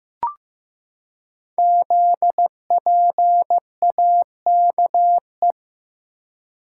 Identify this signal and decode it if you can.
Morse code, heard as a keyed tone, about 15 wpm, 700 Hz. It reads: ZPAKE
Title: Code Groups - 5 Letter:Numbers - 15wpm